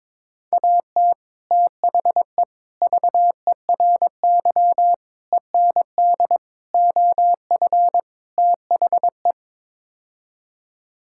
Morse code, heard as a keyed tone, about 22 wpm, 700 Hz. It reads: AT THE VERY END OF THE
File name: practice / 22wpm / Sets of 6 Words - 22wpm